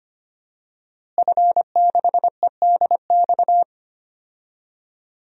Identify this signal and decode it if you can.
Morse code, heard as a keyed tone, about 25 wpm, 700 Hz. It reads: F6EDX
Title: DXCC-100 Call Signs - 5 Characters - Repeated 3x - 25wpm